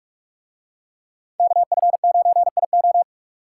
Morse code, heard as a keyed tone, about 44 wpm, 700 Hz. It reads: KF9IO